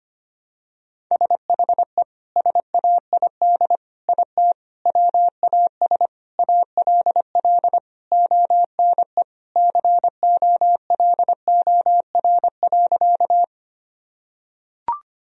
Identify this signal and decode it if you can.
Morse code, heard as a keyed tone, about 25 wpm, 700 Hz. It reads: SHE SAID IT WAS ALL ONE COLOR.
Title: Sentences from Top 1000 Words - Review Part 2 - 25wpm